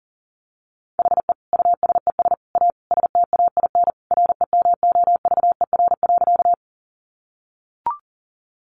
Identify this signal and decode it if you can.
Morse code, heard as a keyed tone, about 40 wpm, 700 Hz. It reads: HE USES A STAIN REMOVER.
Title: Additional Easy Phrases 40wpm